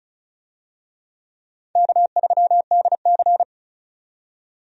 Morse code, heard as a keyed tone, about 34 wpm, 700 Hz. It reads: K3DC
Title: USA Call Signs - 4 Characters - 35wpm